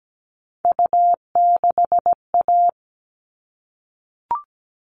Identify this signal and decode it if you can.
Morse code, heard as a keyed tone, about 17 wpm, 700 Hz. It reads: U6A
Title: All Cut Numbers - 3-Digit 17wpm